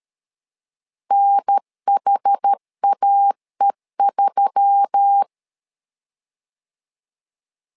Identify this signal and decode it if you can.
Morse code, heard as a keyed tone, about 13 wpm, 800 Hz. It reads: NHAE3
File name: LEZIONI / 5 LBVC / 5 L 12wpm 1913